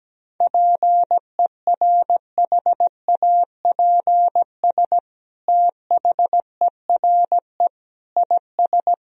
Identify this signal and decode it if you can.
Morse code, heard as a keyed tone, about 17 wpm, 700 Hz. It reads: PERHAPS THERE IS
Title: Sentences from Top 1000 Words - Review Part 4 - 17wpm